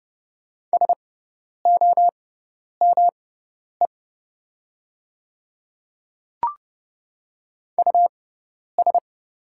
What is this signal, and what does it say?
Morse code, heard as a keyed tone, about 30 wpm, spaced as for 10 wpm, 700 Hz. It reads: SOME US
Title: 6x Character Spacing - Top 200 Words - 30wpm